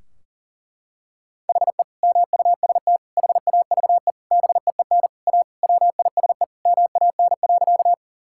Morse code, heard as a keyed tone, about 37 wpm, 700 Hz. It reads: HE MUST HAVE BEEN A WISE MAN.
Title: Sentences from Top 1600 Words 40wpm